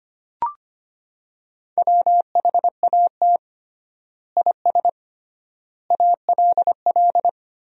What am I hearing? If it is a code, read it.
Morse code, heard as a keyed tone, about 25 wpm, 700 Hz. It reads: WHAT IS ALL
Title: 3x Word-Spacing - Sentences from Top 100 Words - 25wpm